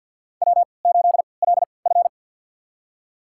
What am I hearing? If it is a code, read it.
Morse code, heard as a keyed tone, about 46 wpm, spaced as for 28 wpm, 700 Hz. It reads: W8LF
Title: USA Call Signs - 4 Characters - 3x Character Spacing - 50wpm